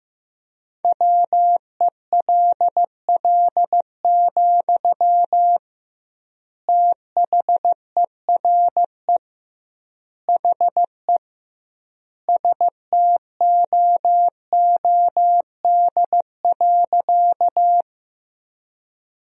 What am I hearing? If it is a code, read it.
Morse code, heard as a keyed tone, about 16 wpm, 700 Hz. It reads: WELL, THERE HE STOOD.
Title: 2x Word-Spacing - Sentences from Top 500 Words - 15wpm